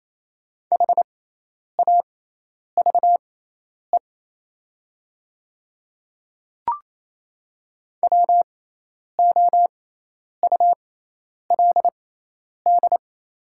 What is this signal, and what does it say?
Morse code, heard as a keyed tone, about 28 wpm, spaced as for 10 wpm, 700 Hz. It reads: HAVE WOULD
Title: 6x Character Spacing - Top 200 Words - 28wpm